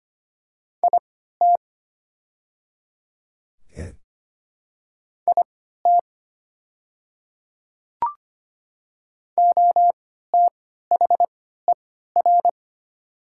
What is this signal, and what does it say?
Morse code, heard as a keyed tone, about 25 wpm, spaced as for 14 wpm, 700 Hz. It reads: IT IT OTHER
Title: 3x Character Spacing - Top 100 Words - 25wpm